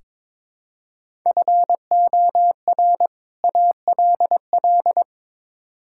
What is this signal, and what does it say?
Morse code, heard as a keyed tone, about 22 wpm, 700 Hz. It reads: FOR ALL
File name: practice / 22wpm / Build A Phrase - 5-Word Phrases - 22wpm